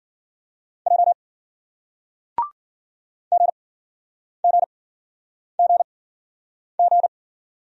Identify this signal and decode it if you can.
Morse code, heard as a keyed tone, about 54 wpm, spaced as for 9 wpm, 700 Hz. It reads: J GGGG